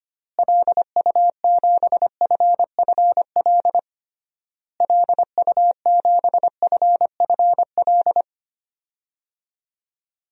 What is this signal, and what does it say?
Morse code, heard as a keyed tone, about 25 wpm, 690 Hz. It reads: LU7FFL LU7FFL
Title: DXCC-100 Call Signs - 6 Characters - Repeated 3x - 25wpm